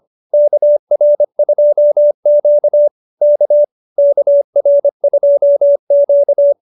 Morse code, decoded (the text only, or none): KR2Q K KR2Q